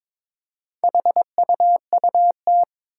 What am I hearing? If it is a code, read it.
Morse code, heard as a keyed tone, about 22 wpm, 700 Hz. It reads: HUUT
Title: Four Challenging Letter-Number Combinations 22wpm